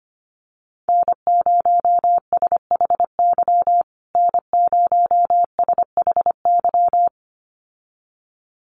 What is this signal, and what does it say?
Morse code, heard as a keyed tone, about 25 wpm, 700 Hz. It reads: N0SHY N0SHY
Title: USA Call Signs Repeated 2x 25wpm